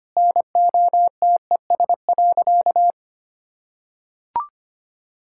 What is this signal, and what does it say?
Morse code, heard as a keyed tone, about 25 wpm, 700 Hz. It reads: NOTES.
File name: Easy Sentences with Hints - 25wpm